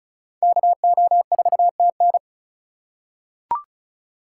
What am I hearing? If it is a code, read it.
Morse code, heard as a keyed tone, about 36 wpm, 700 Hz. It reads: KO4TN
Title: USA Call Signs - 5 Characters - 4x Character Spacing - 35wpm